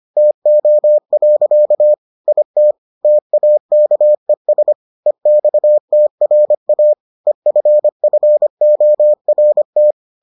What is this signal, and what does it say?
Morse code, heard as a keyed tone, about 25 wpm, 600 Hz. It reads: TO. IT TAKES EXTRA EFFORT